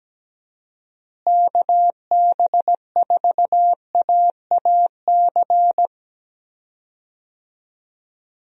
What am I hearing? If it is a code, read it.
Morse code, heard as a keyed tone, about 17 wpm, 700 Hz. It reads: KB4AAC
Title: USA Call Signs - 6 Characters - Repeated 2x - 17wpm